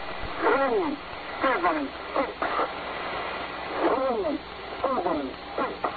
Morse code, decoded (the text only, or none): none